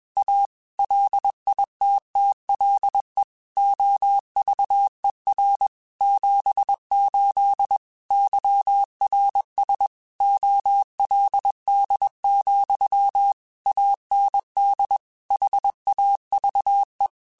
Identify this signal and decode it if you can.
Morse code, heard as a keyed tone, about 21 wpm, 800 Hz. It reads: A LITTLE OVER 78 YRS OLD, AND HAVE
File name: QSOs / QSOs-mp3s / kc7sac@20